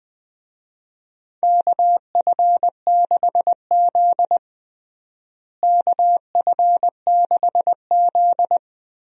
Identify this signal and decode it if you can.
Morse code, heard as a keyed tone, about 20 wpm, 700 Hz. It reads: KF6Z KF6Z
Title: USA Call Signs - 4 Characters - Repeated 2x - 20wpm